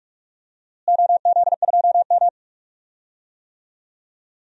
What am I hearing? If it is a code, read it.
Morse code, heard as a keyed tone, about 41 wpm, 700 Hz. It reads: OZ2M
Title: DXCC-100 Call Signs - 4 Characters - Repeated 2x - 45wpm